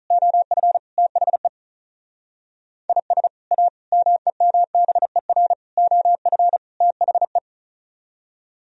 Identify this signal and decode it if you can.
Morse code, heard as a keyed tone, about 45 wpm, 700 Hz. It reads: OF THE IS A MEMBER OF THE